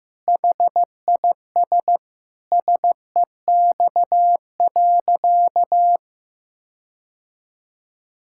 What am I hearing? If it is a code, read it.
Morse code, heard as a keyed tone, about 15 wpm, 700 Hz. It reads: HIS SEX.
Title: Sentences from Top 1600 Words 15wpm